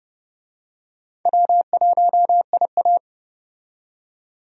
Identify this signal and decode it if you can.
Morse code, heard as a keyed tone, about 32 wpm, 700 Hz. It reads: W1IA
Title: USA Call Signs - 4 Characters - 30wpm